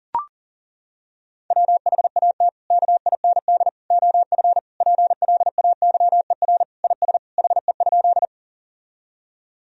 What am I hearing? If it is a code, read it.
Morse code, heard as a keyed tone, about 40 wpm, 700 Hz. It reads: WHAT KIND OF PLAYER IS HE?
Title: Sentences from Top 1000 Words - Review Part 4 - 40wpm